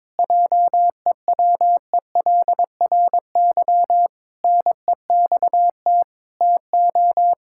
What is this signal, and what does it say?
Morse code, heard as a keyed tone, about 21 wpm, 700 Hz. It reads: JEWELRY NEXT TO
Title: Pangram Challenge v2 - 22wpm